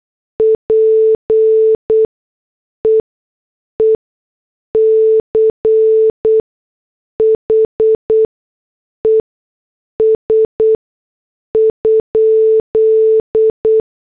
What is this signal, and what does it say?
Morse code, heard as a keyed tone, about 8 wpm, spaced as for 6 wpm, 440 Hz. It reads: PEECHES?